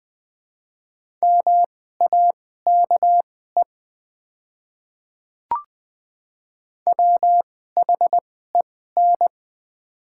Morse code, heard as a keyed tone, about 20 wpm, spaced as for 14 wpm, 690 Hz. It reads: MAKE WHEN
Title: 2x Character Spacing - Top 100 Words - 20wpm